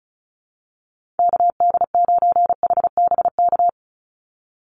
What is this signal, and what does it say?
Morse code, heard as a keyed tone, about 34 wpm, 700 Hz. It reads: KD9HBK
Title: USA Call Signs 35wpm